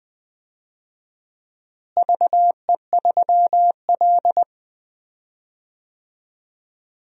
Morse code, heard as a keyed tone, about 20 wpm, 700 Hz. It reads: VE3L
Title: DXCC-100 Call Signs - 4 Characters - Repeated 2x - 20wpm